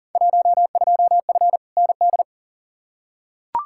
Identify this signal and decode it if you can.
Morse code, heard as a keyed tone, about 40 wpm, 700 Hz. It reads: 12F ND
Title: Field Day Exchanges Encore - 40wpm